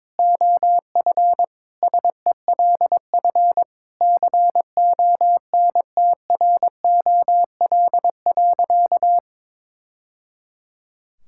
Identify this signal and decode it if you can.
Morse code, heard as a keyed tone, about 22 wpm, 700 Hz. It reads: OF SELF CONTROL.
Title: Sentences from Top 1000 Words - Review Part 5 - 22wpm